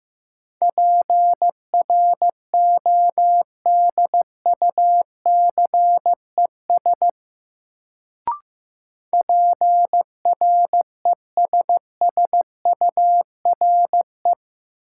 Morse code, heard as a keyed tone, about 15 wpm, 700 Hz. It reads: PRODUCES PRESSURE
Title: Sentences from Top 1000 Words - Review Part 6 - New Word Drill - 15wpm